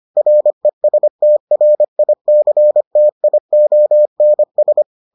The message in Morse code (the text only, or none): RESTRICTIONS